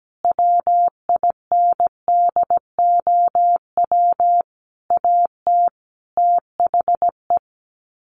Morse code, heard as a keyed tone, about 17 wpm, 700 Hz. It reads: WINDOW AT THE
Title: Sets of 5 Words Encore2 17wpm